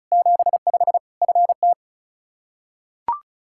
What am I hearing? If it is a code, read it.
Morse code, heard as a keyed tone, about 36 wpm, 700 Hz. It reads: 75 FT
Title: Practice QSO - Exchange 2 - Info - 35wpm